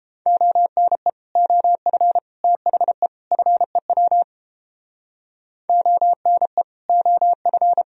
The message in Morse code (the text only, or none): ONE OF THE FEW ONE OF